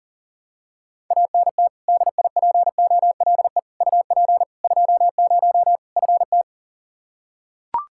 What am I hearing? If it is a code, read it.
Morse code, heard as a keyed tone, about 40 wpm, 700 Hz. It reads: ANT DIPOLE UP 20 FT
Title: Practice QSO - Exchange 2 - Info - 40wpm